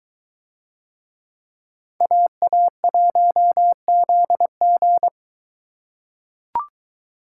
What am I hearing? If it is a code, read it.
Morse code, heard as a keyed tone, about 23 wpm, 700 Hz. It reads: AA1ZG